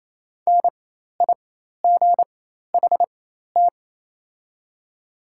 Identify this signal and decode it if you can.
Morse code, heard as a keyed tone, about 28 wpm, spaced as for 13 wpm, 690 Hz. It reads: NIGHT
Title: 4x Character Spacing - Top 300 Words - 28wpm